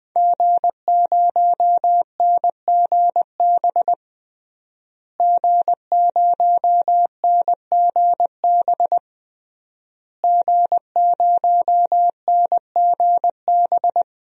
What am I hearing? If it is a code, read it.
Morse code, heard as a keyed tone, about 20 wpm, 700 Hz. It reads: G0NGB G0NGB G0NGB
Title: DXCC-100 Call Signs - 5 Characters - Repeated 3x - 20wpm